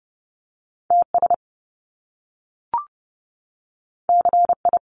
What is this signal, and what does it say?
Morse code, heard as a keyed tone, about 30 wpm, 700 Hz. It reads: TS CI
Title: Top Two Letter Combinations in Words 30wpm